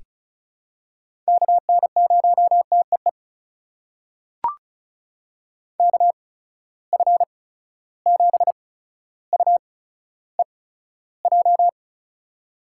Morse code, heard as a keyed tone, about 36 wpm, spaced as for 15 wpm, 700 Hz. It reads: KN0TEE KF7UEJ
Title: USA Call Signs - 6 Characters - 8x Character Spacing - 35wpm